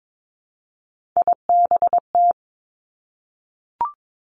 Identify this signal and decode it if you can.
Morse code, heard as a keyed tone, about 22 wpm, 700 Hz. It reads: IBT
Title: Three Challenging Letter-Number Combinations 22wpm